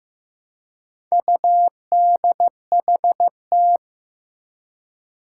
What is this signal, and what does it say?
Morse code, heard as a keyed tone, about 15 wpm, 700 Hz. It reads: UDHT
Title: Four Challenging Letter-Number Combinations 15wpm